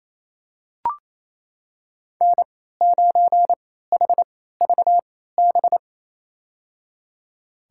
Morse code, heard as a keyed tone, about 27 wpm, spaced as for 16 wpm, 700 Hz. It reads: N9HVB
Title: USA Call Signs - 3x Character Spacing - 28wpm